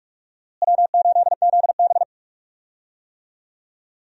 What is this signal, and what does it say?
Morse code, heard as a keyed tone, about 46 wpm, 700 Hz. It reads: W8ZB
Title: USA Call Signs - 4 Characters - Repeated 3x - 45wpm